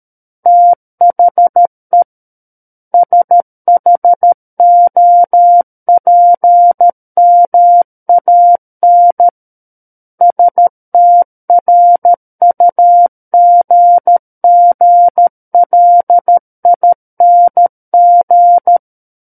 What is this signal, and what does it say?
Morse code, heard as a keyed tone, about 13 wpm, 700 Hz. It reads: THE SHOPMAN STRUGGLING
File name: Book One / 04-Chapter 4 - 13 WPM